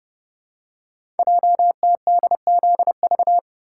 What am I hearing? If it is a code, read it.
Morse code, heard as a keyed tone, about 30 wpm, 700 Hz. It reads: JTDZV
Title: Code Groups - 5 Letters - 30wpm